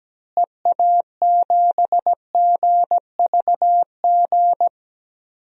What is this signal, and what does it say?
Morse code, heard as a keyed tone, about 17 wpm, 700 Hz. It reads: EA7GVG
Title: DXCC-100 Call Signs - 6 Characters - Repeated 3x - 17wpm